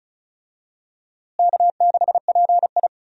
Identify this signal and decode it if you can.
Morse code, heard as a keyed tone, about 36 wpm, 700 Hz. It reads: K6PI
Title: USA Call Signs - 4 Characters - Repeated 2x - 35wpm